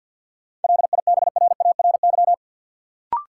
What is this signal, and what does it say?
Morse code, heard as a keyed tone, about 52 wpm, 700 Hz. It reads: LIBRARY
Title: Sentences from Top 600 Words - New Word Drill - 50wpm